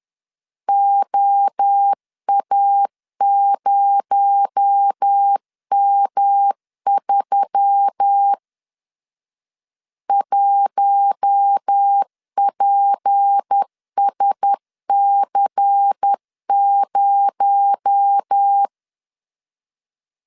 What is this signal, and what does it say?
Morse code, heard as a keyed tone, about 10 wpm, 800 Hz. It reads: OA0M3 1PSC0
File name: LEZIONI / 6 PKX0 / 6 PKX0 10wpm 2707